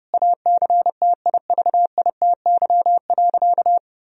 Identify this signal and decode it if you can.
Morse code, heard as a keyed tone, about 30 wpm, 700 Hz. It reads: ACTIVITY.
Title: Sentences from Top 1000 Words - Review Part 4 - 30wpm